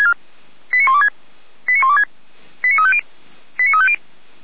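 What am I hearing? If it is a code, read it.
Five-tone selective call (ZVEI), heard as quick runs of five tones, tones of about 70 ms. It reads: 89126 89126 89350 89350
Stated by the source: Received on SDR; ZVEI five-tone call, feuerwehr-orlamuende.org (fire brigade Orlamünde).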